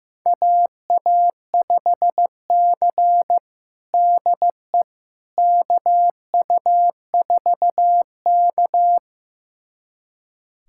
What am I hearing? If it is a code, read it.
Morse code, heard as a keyed tone, about 16 wpm, 700 Hz. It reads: AA5C DE KU4K
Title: USA Callsign DE Callsign - 4 Characters - 15wpm